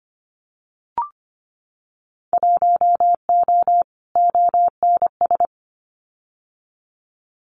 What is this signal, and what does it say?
Morse code, heard as a keyed tone, about 25 wpm, 700 Hz. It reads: 1O ONS
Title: Winter Field Day Exchanges Encore 25wpm